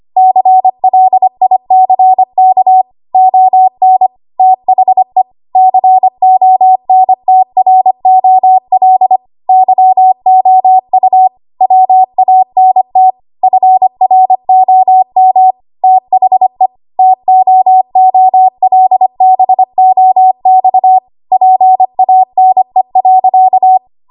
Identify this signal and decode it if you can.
Morse code, heard as a keyed tone, about 25 wpm, 750 Hz. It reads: CLICK ON THE CONTROL YOU WANT FROM THE TOOLBOX PANE.
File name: Archive / 25 WPM / 25WPM